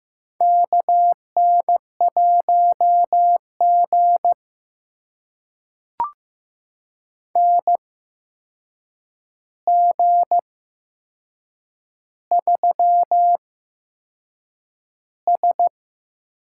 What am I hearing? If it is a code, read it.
Morse code, heard as a keyed tone, about 15 wpm, spaced as for 6 wpm, 700 Hz. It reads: KN1G NG3S
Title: USA Call Signs - 4 Characters - 8x Character Spacing - 15wpm